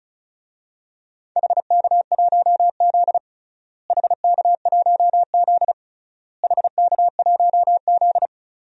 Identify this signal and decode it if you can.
Morse code, heard as a keyed tone, about 36 wpm, 700 Hz. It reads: HK1Z HK1Z HK1Z
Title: DXCC-100 Call Signs - 4 Characters - Repeated 3x - 35wpm